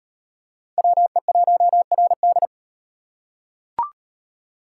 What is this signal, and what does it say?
Morse code, heard as a keyed tone, about 33 wpm, 700 Hz. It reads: WE1RD